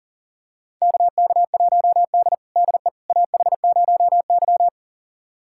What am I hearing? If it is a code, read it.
Morse code, heard as a keyed tone, about 40 wpm, 700 Hz. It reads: KK1D DE AH0Y
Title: USA Callsign DE Callsign - 4 Characters - 40wpm